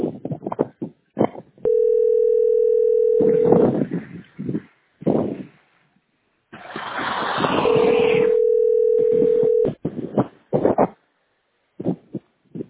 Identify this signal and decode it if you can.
Ringback tone (North America), 440 + 480 Hz.